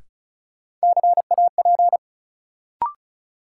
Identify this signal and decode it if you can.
Morse code, heard as a keyed tone, about 36 wpm, 700 Hz. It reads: CAP
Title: Sentences from Top 1600 Words - New Word Drill Alt - 35wpm